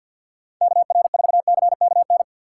Morse code, heard as a keyed tone, about 51 wpm, 700 Hz. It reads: K R 4 C K N
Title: USA Call Signs - 6 Characters - 6x Character Spacing - 50wpm